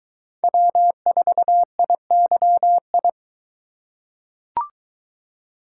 Morse code, heard as a keyed tone, about 22 wpm, 700 Hz. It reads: W4IYI